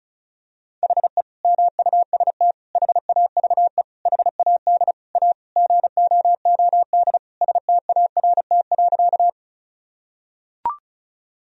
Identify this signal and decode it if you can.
Morse code, heard as a keyed tone, about 36 wpm, 700 Hz. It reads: HE MUST HAVE HAD A GOOD START.
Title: Sentences from Top 1000 Words - Review Part 1 - 35wpm